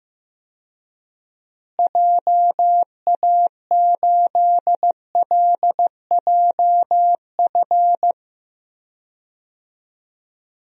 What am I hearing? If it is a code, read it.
Morse code, heard as a keyed tone, about 15 wpm, 700 Hz. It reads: JA8LJF
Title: DXCC-100 Call Signs - 6 Characters - Repeated 3x - 15wpm